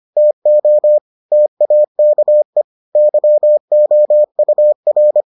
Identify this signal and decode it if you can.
Morse code, heard as a keyed tone, about 25 wpm, 600 Hz. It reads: TO TAKE YOUR